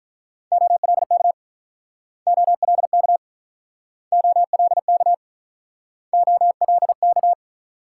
Morse code, heard as a keyed tone, about 46 wpm, 700 Hz. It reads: OLK OLK OLK OLK